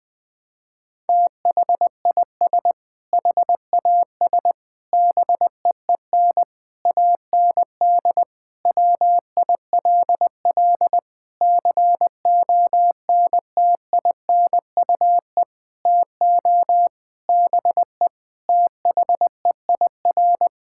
Morse code, heard as a keyed tone, about 21 wpm, 700 Hz. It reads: THIS HAS BEEN AND WILL CONTINUE TO BE THEIR
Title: Sentences from Top 1600 Words 20wpm